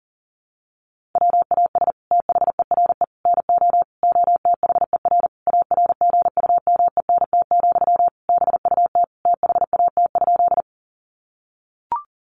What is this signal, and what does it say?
Morse code, heard as a keyed tone, about 40 wpm, 700 Hz. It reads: WAS THERE NO OTHER ARGUMENT, BUT THAT?
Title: Sentences from Top 1400 Words 40wpm